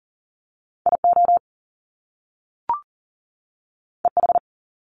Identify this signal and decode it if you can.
Morse code, heard as a keyed tone, about 38 wpm, 700 Hz. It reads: IO EH
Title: Top Two Letter Combinations in Words 40wpm